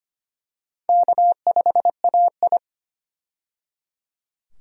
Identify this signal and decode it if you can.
Morse code, heard as a keyed tone, about 25 wpm, 700 Hz. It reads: K5AI
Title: USA Call Signs - 4 Characters - 25wpm